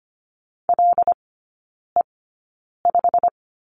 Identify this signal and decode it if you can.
Morse code, heard as a keyed tone, about 25 wpm, spaced as for 9 wpm, 700 Hz. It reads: LE5
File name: Lesson 15e - Number 5 - Three Characters 25wpm